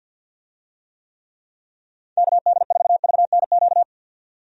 Morse code, heard as a keyed tone, about 51 wpm, 700 Hz. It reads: KD4VNQ
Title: USA Call Signs - 6 Characters - 50wpm